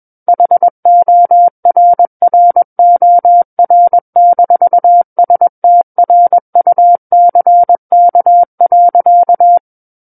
Morse code, heard as a keyed tone, about 21 wpm, 700 Hz. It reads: HORROR-STRUCK.
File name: Book One / 12-Chapter 12 - 21 WPM